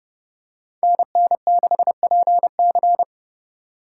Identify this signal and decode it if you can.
Morse code, heard as a keyed tone, about 30 wpm, 700 Hz. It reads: NN6PC
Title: USA Call Signs - 5 Characters - 3x Character Spacing - 30wpm